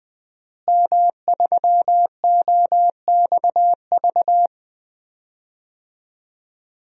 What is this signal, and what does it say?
Morse code, heard as a keyed tone, about 20 wpm, 700 Hz. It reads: M3OXV
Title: DXCC-100 Call Signs - 5 Characters - Repeated 3x - 20wpm